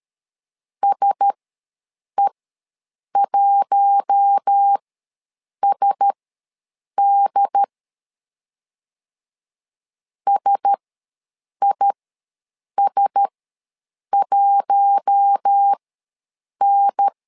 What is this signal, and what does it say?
Morse code, heard as a keyed tone, about 13 wpm, spaced as for 7 wpm, 800 Hz. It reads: SE1SD SIS1N
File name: LEZIONI / 4 Q1S ws 12wpm 1827